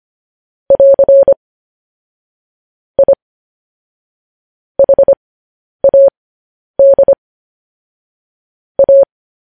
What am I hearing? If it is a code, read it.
Morse code, heard as a keyed tone, about 25 wpm, spaced as for 10 wpm, 550 Hz. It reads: <AR> I HAD A
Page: news headlines in morse code at 10 wpm